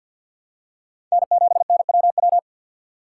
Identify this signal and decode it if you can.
Morse code, heard as a keyed tone, about 50 wpm, 700 Hz. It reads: N7NWW
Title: USA Call Signs - 5 Characters - 6x Character Spacing - 50wpm